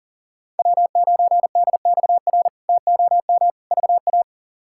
Morse code, heard as a keyed tone, about 40 wpm, 700 Hz. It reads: W9DXR TOM VA
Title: K1USN SST QSO Practice - 40wpm